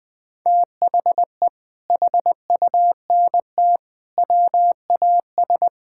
Morse code, heard as a keyed tone, about 20 wpm, 700 Hz. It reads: THE HUNT WAS
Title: Sentences from Top 1600 Words 20wpm